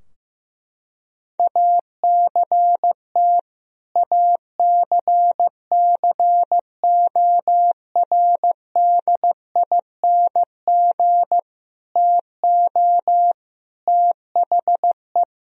Morse code, heard as a keyed tone, about 15 wpm, 700 Hz. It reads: ACT ACCORDING TO THE
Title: Sentences from Top 1000 Words - Review Part 4 - 15wpm